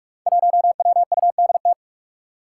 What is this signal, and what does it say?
Morse code, heard as a keyed tone, about 44 wpm, 700 Hz. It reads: 1WUDT
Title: Five Challenging Letter-Number Combinations 45wpm